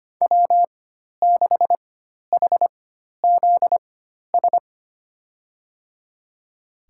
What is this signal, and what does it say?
Morse code, heard as a keyed tone, about 25 wpm, spaced as for 12 wpm, 700 Hz. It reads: W6HZS
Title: USA Call Signs - 5 Characters - 4x Character Spacing - 25wpm